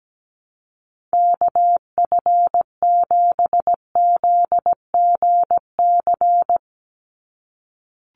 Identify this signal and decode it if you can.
Morse code, heard as a keyed tone, about 17 wpm, 700 Hz. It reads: KF7ZGC
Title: USA Call Signs 17wpm